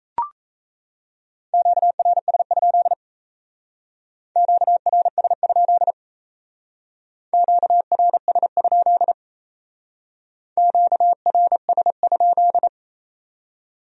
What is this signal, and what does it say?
Morse code, heard as a keyed tone, about 35 wpm, 700 Hz. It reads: QRS? QRS? QRS? QRS?
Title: Speed-Racing Q-Codes - 28wpm